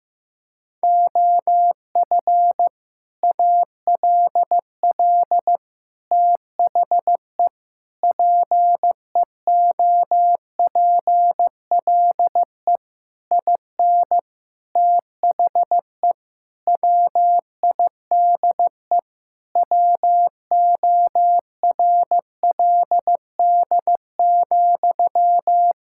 Morse code, heard as a keyed tone, about 16 wpm, 700 Hz. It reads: OF ALL THE PEOPLE IN THE WIDE WORLD,
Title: Sentences from Top 1000 Words - Review Part 6 - 15wpm